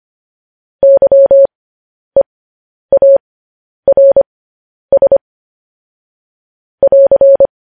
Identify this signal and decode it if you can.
Morse code, heard as a keyed tone, about 25 wpm, spaced as for 10 wpm, 550 Hz. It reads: YEARS <AR>
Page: News Headlines in Morse Code at 10 WPM